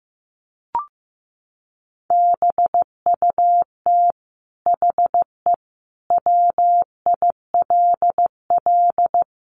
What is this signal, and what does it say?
Morse code, heard as a keyed tone, about 15 wpm, 700 Hz. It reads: BUT HE WILL
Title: Sentences from Top 1200 Words 15wpm